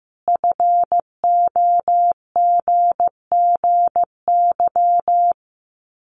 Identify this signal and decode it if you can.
Morse code, heard as a keyed tone, about 15 wpm, 700 Hz. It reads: FOGGY
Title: Common QSO Phrases 15wpm